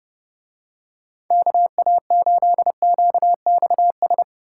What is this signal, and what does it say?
Morse code, heard as a keyed tone, about 31 wpm, 700 Hz. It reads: KA8QXS